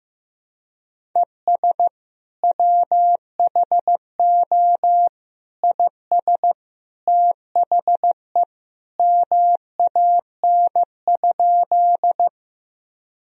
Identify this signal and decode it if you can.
Morse code, heard as a keyed tone, about 16 wpm, 700 Hz. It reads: ES WHO IS THE MAN?
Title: Sentences from Top 200 Words and Abbreviations 15wpm